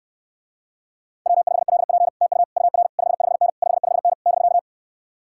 Morse code, heard as a keyed tone, about 69 wpm, 700 Hz. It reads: W5LP TU UR 55N 55N <BK>